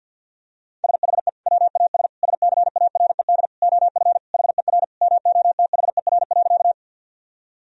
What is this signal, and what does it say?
Morse code, heard as a keyed tone, about 50 wpm, 700 Hz. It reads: SHE WAS SCARED OF HER MOTHER.
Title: Sentences from Top 1000 Words - Review Part 4 - 50wpm